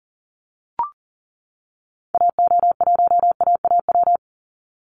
Morse code, heard as a keyed tone, about 40 wpm, 700 Hz. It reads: AO1AAW
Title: Call Signs - 40wpm